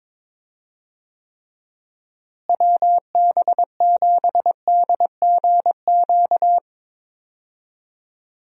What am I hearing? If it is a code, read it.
Morse code, heard as a keyed tone, about 22 wpm, 700 Hz. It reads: WB7DGQ
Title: USA Call Signs - 6 Characters - Repeated 2x - 22wpm